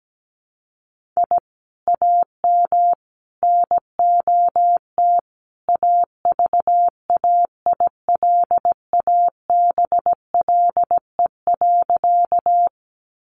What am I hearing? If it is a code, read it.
Morse code, heard as a keyed tone, about 17 wpm, 700 Hz. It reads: I AM NOT AVAILABLE.
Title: Sentences from Top 600 Words 17wpm